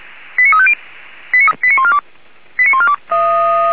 Five-tone selective call (ZVEI), heard as quick runs of five tones, tones of about 80 ms. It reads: 89350 89142 89142